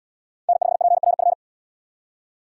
Morse code, heard as a keyed tone, about 77 wpm, 700 Hz. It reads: N5PRF